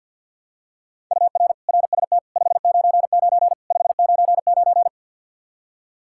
Morse code, heard as a keyed tone, about 51 wpm, 700 Hz. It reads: UR RST 599 599
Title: Practice QSO - Exchange 1 - Beginning - 50wpm